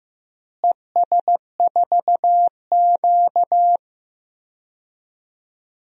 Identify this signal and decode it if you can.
Morse code, heard as a keyed tone, about 15 wpm, 700 Hz. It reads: ES4Q
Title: DXCC-100 Call Signs - 4 Characters - Repeated 2x - 15wpm